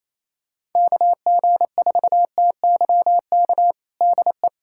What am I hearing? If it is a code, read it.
Morse code, heard as a keyed tone, about 28 wpm, 700 Hz. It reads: KG4TYK DE
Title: USA Callsign DE Callsign - 6 Characters - 28wpm